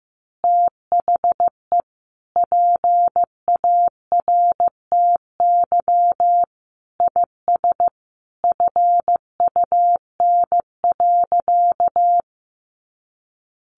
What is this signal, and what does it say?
Morse code, heard as a keyed tone, about 15 wpm, 700 Hz. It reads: THE PARTY IS FUN.
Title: Easy Phrases 15wpm